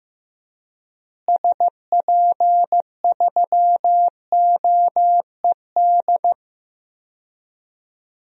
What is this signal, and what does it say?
Morse code, heard as a keyed tone, about 15 wpm, 700 Hz. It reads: SP3OED
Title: DXCC-100 Call Signs - 6 Characters - Repeated 3x - 15wpm